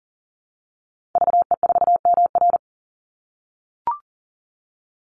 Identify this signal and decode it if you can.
Morse code, heard as a keyed tone, about 40 wpm, 700 Hz. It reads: VE4MR
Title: Call Signs - 40wpm